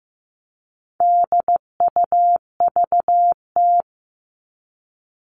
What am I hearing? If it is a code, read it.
Morse code, heard as a keyed tone, about 15 wpm, 700 Hz. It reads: DUVT
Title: All Cut Numbers - 4-Digit 15wpm